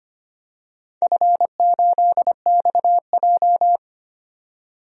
Morse code, heard as a keyed tone, about 25 wpm, 700 Hz. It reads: F8XJ
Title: DXCC-100 Call Signs - 4 Characters - 25wpm